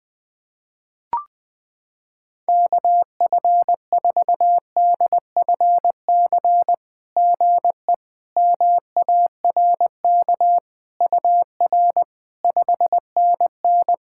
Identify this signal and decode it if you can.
Morse code, heard as a keyed tone, about 20 wpm, 700 Hz. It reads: KF4DFC GE MARK UR 5NN
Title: SOTA QSO Practice 20wpm